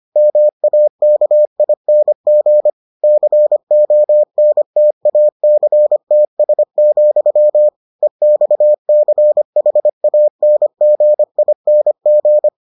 Morse code, heard as a keyed tone, about 25 wpm, 600 Hz. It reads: MAKING CONTACTS, EXCHANGING